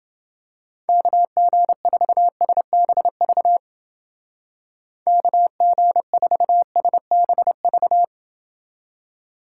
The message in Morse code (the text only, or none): KG4SBV KG4SBV